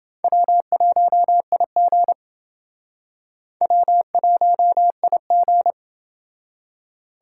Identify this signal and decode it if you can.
Morse code, heard as a keyed tone, about 30 wpm, 700 Hz. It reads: W1IG W1IG